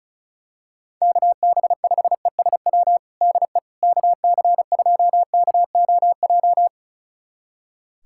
Morse code, heard as a keyed tone, about 36 wpm, 700 Hz. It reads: KB5ESW DE KC2KOJ
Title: USA Callsign DE Callsign 35wpm